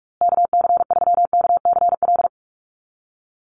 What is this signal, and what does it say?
Morse code, heard as a keyed tone, about 44 wpm, 700 Hz. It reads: KC3KCL
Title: USA Call Signs Repeated 3x 45wpm